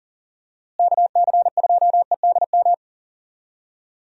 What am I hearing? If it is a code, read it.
Morse code, heard as a keyed tone, about 40 wpm, 700 Hz. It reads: KC2EDM
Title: USA Call Signs - 6 Characters - Repeated 2x - 40wpm